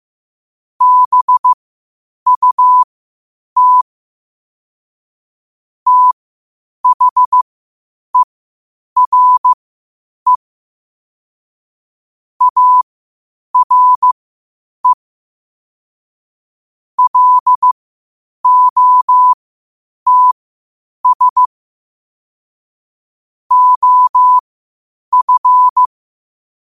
Morse code, heard as a keyed tone, about 15 wpm, spaced as for 8 wpm, 1000 Hz. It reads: BUT THERE ARE LOTS OF
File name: qotdfiles / 8wpm